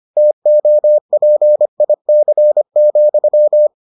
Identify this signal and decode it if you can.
Morse code, heard as a keyed tone, about 25 wpm, 600 Hz. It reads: TOPIC,